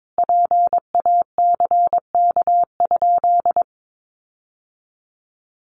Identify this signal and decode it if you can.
Morse code, heard as a keyed tone, about 22 wpm, 700 Hz. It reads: PACK?
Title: Sentences from Top 1300 Words 22wpm